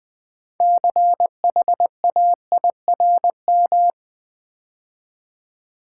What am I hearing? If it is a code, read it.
Morse code, heard as a keyed tone, about 20 wpm, 700 Hz. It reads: CHAIRM
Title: Build A Word - 3 Letters At A Time - 8-Letter Words - 20wpm